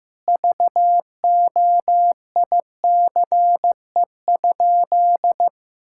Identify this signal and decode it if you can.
Morse code, heard as a keyed tone, about 15 wpm, 700 Hz. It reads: VOICE?
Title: Sentences from Top 1600 Words 15wpm